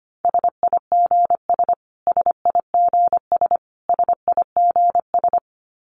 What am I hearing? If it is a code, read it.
Morse code, heard as a keyed tone, about 25 wpm, 700 Hz. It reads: SIGS SIGS SIGS
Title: Abbreviations Repeated 3x - 25wpm